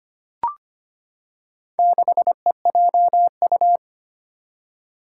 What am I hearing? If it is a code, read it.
Morse code, heard as a keyed tone, about 25 wpm, 700 Hz. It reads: 6EJU
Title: Four Challenging Letter-Number Combinations 25wpm